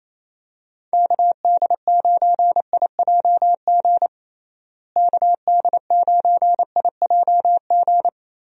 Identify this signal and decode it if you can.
Morse code, heard as a keyed tone, about 28 wpm, 700 Hz. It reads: KD9IJG KD9IJG
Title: USA Call Signs - 6 Characters - Repeated 3x - 28wpm